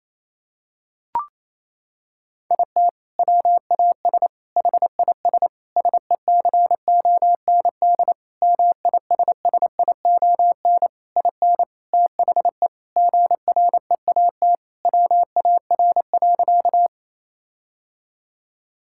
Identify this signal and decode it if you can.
Morse code, heard as a keyed tone, about 28 wpm, 700 Hz. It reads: IT WAS HIS SECOND MISSION IN THE GREAT WAR.
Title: Sentences from Top 1600 Words 28wpm